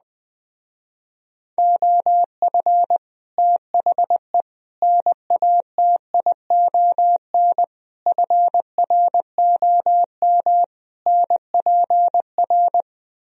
Morse code, heard as a keyed tone, about 20 wpm, 700 Hz. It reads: OF THE NATION FROM NPR
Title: Speed-Racing - Sets of 5 Words 17wpm